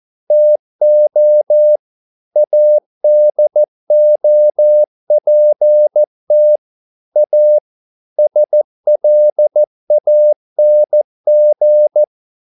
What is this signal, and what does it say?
Morse code, heard as a keyed tone, about 14 wpm, 600 Hz. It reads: TO ADOPT A SLANG
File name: mp3 files / a dogs mistake - 14wpm